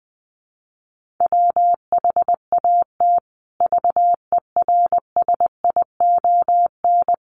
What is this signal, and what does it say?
Morse code, heard as a keyed tone, about 20 wpm, 700 Hz. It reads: WHAT VERSION
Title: Sentences from Top 1400 Words 20wpm